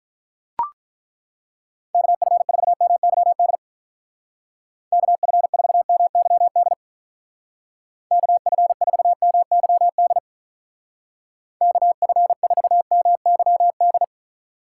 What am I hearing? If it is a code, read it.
Morse code, heard as a keyed tone, about 41 wpm, 700 Hz. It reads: KF4MYD KF4MYD KF4MYD KF4MYD
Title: Speed Racing - USA 2x3 Call Signs 35wpm